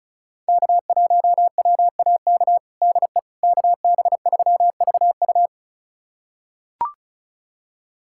Morse code, heard as a keyed tone, about 36 wpm, 700 Hz. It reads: K1WAK DE KB3VU
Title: USA Callsign DE Callsign - 5 Characters - 35wpm